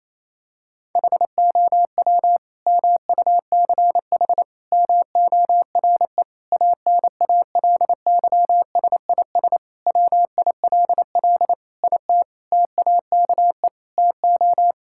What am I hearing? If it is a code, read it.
Morse code, heard as a keyed tone, about 28 wpm, 700 Hz. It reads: HOW MUCH MORE ANALYSIS WILL IT TAKE TO
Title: Sentences from Top 1000 Words - Review Part 5 - 28wpm